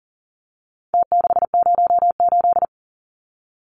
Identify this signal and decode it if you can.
Morse code, heard as a keyed tone, about 40 wpm, 700 Hz. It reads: T608
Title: Numbers - Contest Style - 4-Digit 40wpm